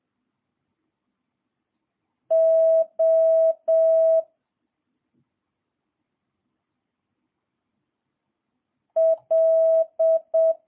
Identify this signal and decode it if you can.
Morse code, heard as a keyed tone, about 7 wpm, 650 Hz. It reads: O L
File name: jh8cbh / 2022cw